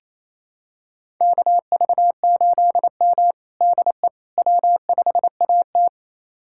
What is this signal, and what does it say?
Morse code, heard as a keyed tone, about 28 wpm, 700 Hz. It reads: KV8M DE W5AT
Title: USA Callsign DE Callsign - 4 Characters - 28wpm